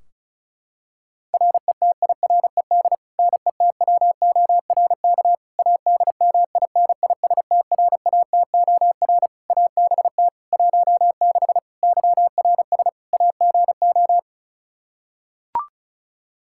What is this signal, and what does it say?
Morse code, heard as a keyed tone, about 36 wpm, 700 Hz. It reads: RETIRED NETWORK ADMINISTRATOR ABT 16 YRS AGO
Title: Practice QSO - Exchange 3 - More Info - 35wpm